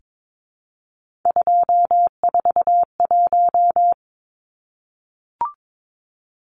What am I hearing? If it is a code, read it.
Morse code, heard as a keyed tone, about 22 wpm, 700 Hz. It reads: 241
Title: Numbers - 3-Digit 22wpm